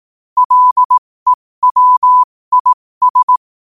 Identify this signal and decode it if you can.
Morse code, heard as a keyed tone, about 18 wpm, spaced as for 15 wpm, 1000 Hz. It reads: LEWIS
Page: New quotes every day in morse code at 15 Words per minute.